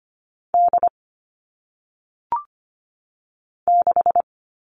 Morse code, heard as a keyed tone, about 25 wpm, 700 Hz. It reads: D 6
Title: All Cut Numbers - 1-Digit 25wpm